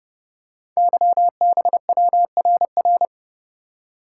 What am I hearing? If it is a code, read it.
Morse code, heard as a keyed tone, about 29 wpm, 700 Hz. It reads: YBWRR
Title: Code Groups - 5 Letters - 30wpm